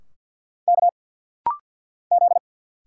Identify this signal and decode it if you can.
Morse code, heard as a keyed tone, about 52 wpm, 700 Hz. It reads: K Z
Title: Single Letters - Rapid-Fire 50wpm